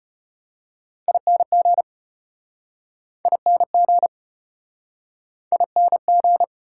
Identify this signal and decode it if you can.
Morse code, heard as a keyed tone, about 36 wpm, 700 Hz. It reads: ING ING ING